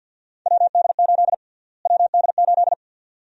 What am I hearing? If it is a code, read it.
Morse code, heard as a keyed tone, about 53 wpm, 700 Hz. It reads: WD8 WD8
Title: USA Call Sign Prefixes - Repeated 2x - 50wpm